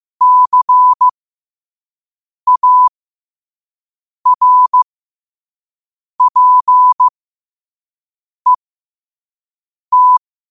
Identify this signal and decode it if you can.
Morse code, heard as a keyed tone, about 15 wpm, spaced as for 5 wpm, 1000 Hz. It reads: CARPET
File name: qotdfiles / 5wpm